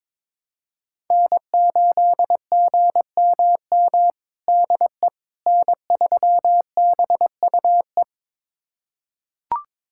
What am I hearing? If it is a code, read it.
Morse code, heard as a keyed tone, about 22 wpm, 700 Hz. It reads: N8GMM DE N3BUE
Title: USA Callsign DE Callsign - 5 Characters - 22wpm